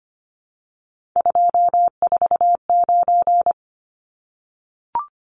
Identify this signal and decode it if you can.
Morse code, heard as a keyed tone, about 25 wpm, 700 Hz. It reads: 249
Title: Numbers - 3-Digit 25wpm